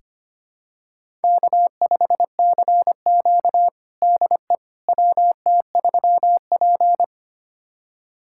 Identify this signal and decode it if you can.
Morse code, heard as a keyed tone, about 25 wpm, 700 Hz. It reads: K5CQ DE WT3P
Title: USA Callsign DE Callsign - 4 Characters - 25wpm